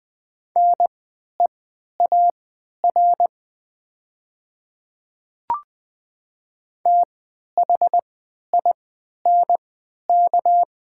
Morse code, heard as a keyed tone, about 20 wpm, spaced as for 11 wpm, 700 Hz. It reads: NEAR THINK
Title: 3x Character Spacing - Top 300 Words - 20wpm